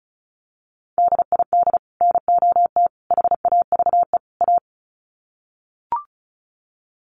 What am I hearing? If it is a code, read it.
Morse code, heard as a keyed tone, about 34 wpm, 700 Hz. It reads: DID NOT HAVE A
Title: Sets of 4 Words 35wpm